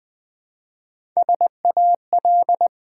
Morse code, heard as a keyed tone, about 20 wpm, 700 Hz. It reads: SAL
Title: Build A Word - 1 Letter At A Time - 3 to 5-Letter Words - 20wpm